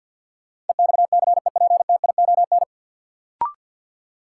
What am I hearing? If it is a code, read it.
Morse code, heard as a keyed tone, about 53 wpm, 700 Hz. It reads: EXCEPTION
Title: Sets of 4 Words - New Word Drill - 50wpm